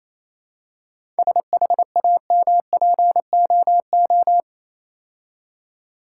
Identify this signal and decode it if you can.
Morse code, heard as a keyed tone, about 26 wpm, 690 Hz. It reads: SHAMPOO
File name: Easy Words - Speed-Racing with Hints - 22wpm